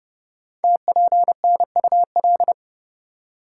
Morse code, heard as a keyed tone, about 29 wpm, 700 Hz. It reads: TPNUL
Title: Code Groups - 5 Letters - 30wpm